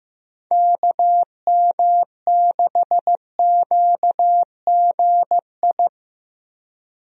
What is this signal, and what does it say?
Morse code, heard as a keyed tone, about 15 wpm, 700 Hz. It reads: KM6QGI
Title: USA Call Signs - 6 Characters - Repeated 2x - 15wpm